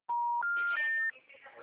Special information tone, with noise.
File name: out- -2012- - - .41397